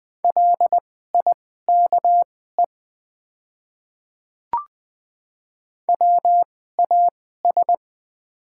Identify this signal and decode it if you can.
Morse code, heard as a keyed tone, about 20 wpm, spaced as for 14 wpm, 700 Hz. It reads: LIKE WAS
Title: 2x Character Spacing - Top 100 Words - 20wpm